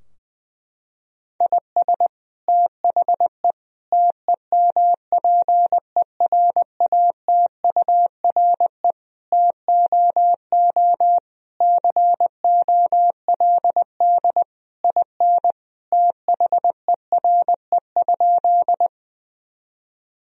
Morse code, heard as a keyed tone, about 20 wpm, 700 Hz. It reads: IS THE TEMPERATURE TOO COLD IN THERE?
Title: Sentences from Top 1000 Words - Review Part 4 - 20wpm